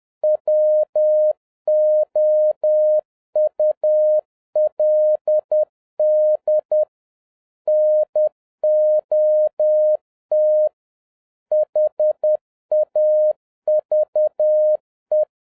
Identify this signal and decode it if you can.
Morse code, heard as a keyed tone, about 10 wpm, 610 Hz. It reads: WOULD NOT HAVE